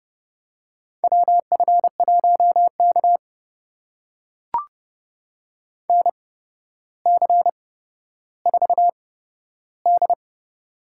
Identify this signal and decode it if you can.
Morse code, heard as a keyed tone, about 30 wpm, spaced as for 13 wpm, 700 Hz. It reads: WF1K NC4D
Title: USA Call Signs - 4 Characters - 8x Character Spacing - 30wpm